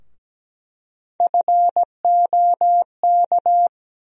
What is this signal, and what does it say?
Morse code, heard as a keyed tone, about 17 wpm, 700 Hz. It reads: FOK
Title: USA Call Sign Postfixes - 3 Letters 17wpm